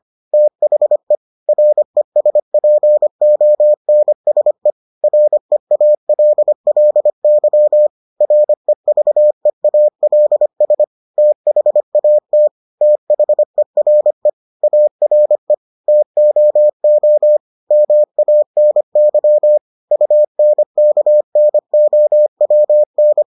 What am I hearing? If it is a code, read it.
Morse code, heard as a keyed tone, about 25 wpm, 600 Hz. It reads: THE RESPONSE REALLY REVEALS THAT THERE ARE TOO MANY UNKNOWN